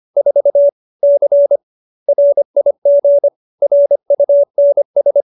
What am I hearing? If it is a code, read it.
Morse code, heard as a keyed tone, about 25 wpm, 570 Hz. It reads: 4 C RIG RUNS
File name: QSO315 25